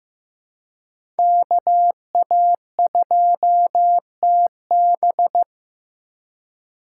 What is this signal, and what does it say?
Morse code, heard as a keyed tone, about 15 wpm, 700 Hz. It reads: KA2TB
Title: Code Groups - 5 Letter:Numbers - 15wpm